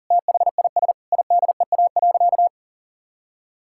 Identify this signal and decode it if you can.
Morse code, heard as a keyed tone, about 40 wpm, 700 Hz. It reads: THIS IDEA.
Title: Sentences from Top 1600 Words 40wpm